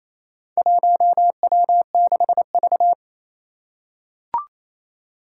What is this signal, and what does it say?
Morse code, heard as a keyed tone, about 28 wpm, 700 Hz. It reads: 1W6V
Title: Four Challenging Letter-Number Combinations 28wpm